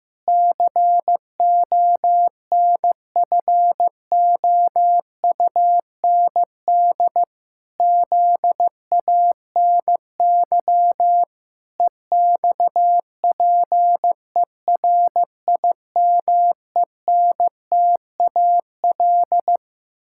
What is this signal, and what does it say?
Morse code, heard as a keyed tone, about 14 wpm, 700 Hz. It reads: CONFOUND ZANY EXPERIMENTAL
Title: Pangram Challenge v2 - 15wpm